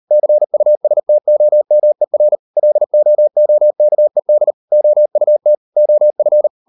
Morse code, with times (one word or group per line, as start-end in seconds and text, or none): 0.1-2.4 CUSTOMER
2.6-4.5 LOOKED
4.7-5.5 OUT
5.8-6.5 OF